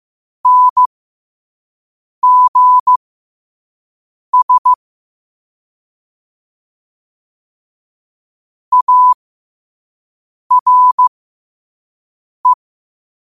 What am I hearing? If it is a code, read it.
Morse code, heard as a keyed tone, about 15 wpm, spaced as for 5 wpm, 1000 Hz. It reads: NGS ARE
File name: qotdfiles / 5wpm